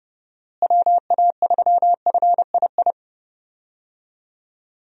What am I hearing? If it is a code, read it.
Morse code, heard as a keyed tone, about 30 wpm, 700 Hz. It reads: WA3FII
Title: USA Call Signs - 6 Characters - Repeated 2x - 30wpm